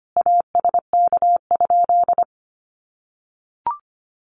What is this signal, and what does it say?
Morse code, heard as a keyed tone, about 25 wpm, 700 Hz. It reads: ASK?
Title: Sentences from Top 200 Words 25wpm